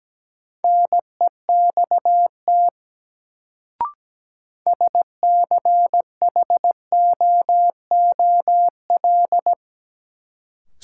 Morse code, heard as a keyed tone, about 17 wpm, 700 Hz. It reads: NEXT SCHOOL
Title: Sentences from Top 300 Words - New Word Drill - 17wpm